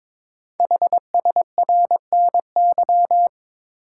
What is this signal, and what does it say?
Morse code, heard as a keyed tone, about 22 wpm, 700 Hz. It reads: HSRNY
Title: Code Groups - 5 Letters - 22wpm